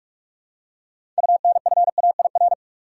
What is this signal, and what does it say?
Morse code, heard as a keyed tone, about 46 wpm, 700 Hz. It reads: UNFAIR
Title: Six Letter Words - 45wpm